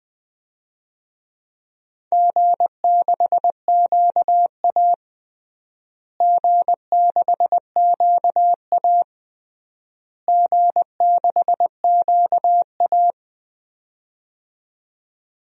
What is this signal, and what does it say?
Morse code, heard as a keyed tone, about 20 wpm, 700 Hz. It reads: G6QA G6QA G6QA
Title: DXCC-100 Call Signs - 4 Characters - Repeated 3x - 20wpm